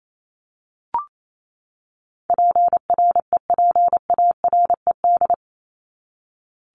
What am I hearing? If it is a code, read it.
Morse code, heard as a keyed tone, about 28 wpm, 700 Hz. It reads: PREPARED
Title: ED Suffix - Top 100 Words with Base Words 28wpm